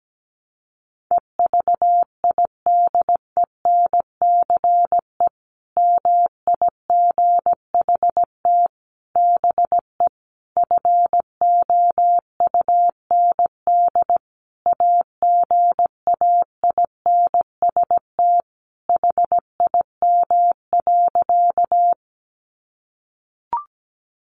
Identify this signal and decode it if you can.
Morse code, heard as a keyed tone, about 17 wpm, 700 Hz. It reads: EVIDENCE MIGHT BE FOUND AGAINST HIM.
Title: Sentences from Top 1300 Words 17wpm